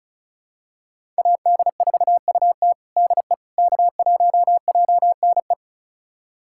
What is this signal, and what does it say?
Morse code, heard as a keyed tone, about 34 wpm, 700 Hz. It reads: AD4UT DE K1JNE
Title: USA Callsign DE Callsign - 5 Characters - 35wpm